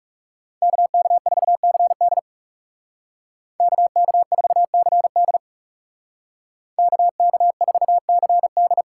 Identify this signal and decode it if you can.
Morse code, heard as a keyed tone, about 37 wpm, 700 Hz. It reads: KK4CD KK4CD KK4CD
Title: Speed Racing - USA Call Signs 30wpm